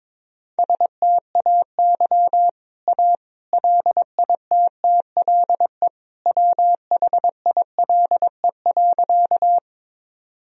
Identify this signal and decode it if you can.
Morse code, heard as a keyed tone, about 22 wpm, 700 Hz. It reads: STAY A LITTLE WHILE.
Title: Sentences from Top 1000 Words - Review Part 1 - 22wpm